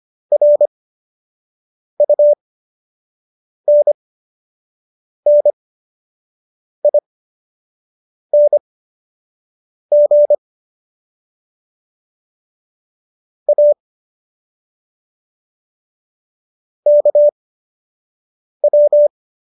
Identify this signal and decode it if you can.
Morse code, heard as a keyed tone, about 25 wpm, spaced as for 6 wpm, 590 Hz. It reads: RUNNING A KW